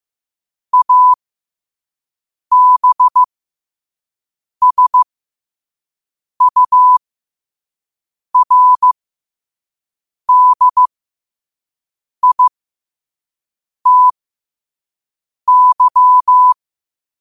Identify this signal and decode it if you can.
Morse code, heard as a keyed tone, about 15 wpm, spaced as for 5 wpm, 1000 Hz. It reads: ABSURDITY
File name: qotdfiles / 5wpm